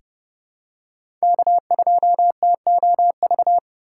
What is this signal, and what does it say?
Morse code, heard as a keyed tone, about 30 wpm, 700 Hz. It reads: K2TOV
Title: USA Call Signs - 5 Characters - 6x Character Spacing - 30wpm